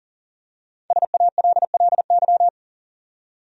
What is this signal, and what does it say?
Morse code, heard as a keyed tone, about 40 wpm, 700 Hz. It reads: SALLY
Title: USA Common Names - 40wpm